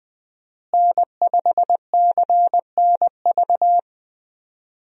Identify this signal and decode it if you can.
Morse code, heard as a keyed tone, about 20 wpm, 700 Hz. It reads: N5CNV
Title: USA Call Signs - 5 Characters - Repeated 3x - 20wpm